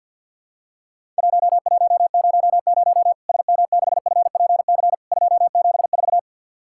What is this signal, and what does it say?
Morse code, heard as a keyed tone, about 50 wpm, 700 Hz. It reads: 1100 SM6FPC 274
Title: Worked All Europe DX Contest - QTC Exchange with Repeats - 50wpm